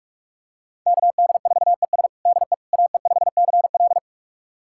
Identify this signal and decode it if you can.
Morse code, heard as a keyed tone, about 46 wpm, 700 Hz. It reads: KD4ES DE AE5CL
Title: USA Callsign DE Callsign - 5 Characters - 45wpm